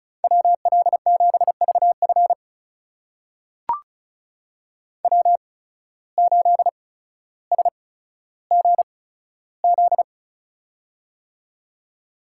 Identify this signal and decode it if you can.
Morse code, heard as a keyed tone, about 34 wpm, spaced as for 15 wpm, 690 Hz. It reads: WL7VF W8SGZ
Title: USA Call Signs - 5 Characters - 8x Character Spacing - 35wpm